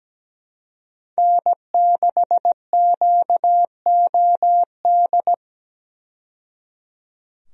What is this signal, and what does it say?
Morse code, heard as a keyed tone, about 17 wpm, 700 Hz. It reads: N6QOD
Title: USA Call Signs - 5 Characters - 17wpm